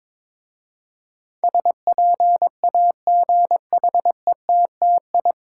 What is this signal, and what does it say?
Morse code, heard as a keyed tone, about 22 wpm, 700 Hz. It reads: SPAGHETTI
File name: Difficult Words with Hints - 22wpm